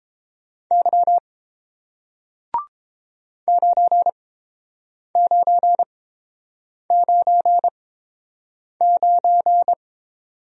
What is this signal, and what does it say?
Morse code, heard as a keyed tone, about 29 wpm, spaced as for 7 wpm, 690 Hz. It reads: Y 9999